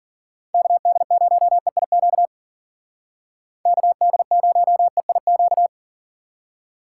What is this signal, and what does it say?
Morse code, heard as a keyed tone, about 52 wpm, 700 Hz. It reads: KD0EIQ KD0EIQ